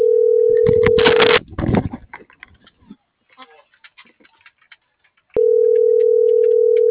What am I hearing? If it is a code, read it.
Ringback tone (North America), 440 + 480 Hz.